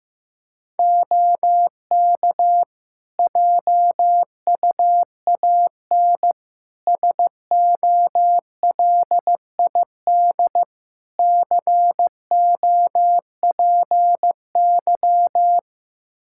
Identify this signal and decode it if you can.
Morse code, heard as a keyed tone, about 15 wpm, 700 Hz. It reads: OK JUAN SOLID COPY
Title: K1USN SST QSO Practice 15wpm